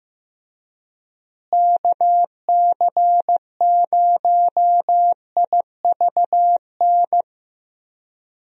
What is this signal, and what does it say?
Morse code, heard as a keyed tone, about 15 wpm, 700 Hz. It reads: KC0IVN
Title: USA Call Signs - 6 Characters - Repeated 2x - 15wpm